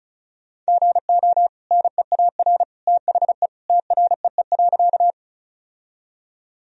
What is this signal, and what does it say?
Morse code, heard as a keyed tone, about 34 wpm, 700 Hz. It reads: GO NEAR THE TREE.
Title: Sentences from Top 1000 Words - Review Part 1 - 35wpm